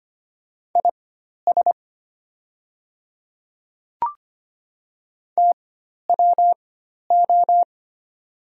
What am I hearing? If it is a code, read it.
Morse code, heard as a keyed tone, about 25 wpm, spaced as for 12 wpm, 700 Hz. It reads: IS TWO
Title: 4x Character Spacing - Top 100 Words - 25wpm